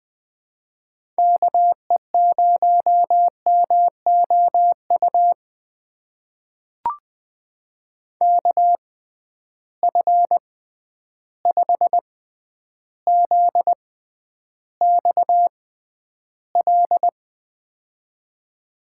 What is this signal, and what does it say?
Morse code, heard as a keyed tone, about 20 wpm, spaced as for 10 wpm, 700 Hz. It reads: KE0MOU KF5ZXL
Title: USA Call Signs - 6 Characters - 6x Character Spacing - 20wpm